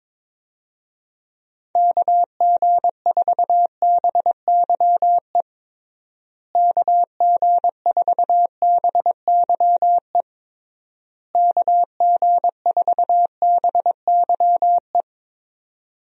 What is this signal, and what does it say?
Morse code, heard as a keyed tone, about 22 wpm, 700 Hz. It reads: KG4BYE KG4BYE KG4BYE
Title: USA Call Signs - 6 Characters - Repeated 3x - 22wpm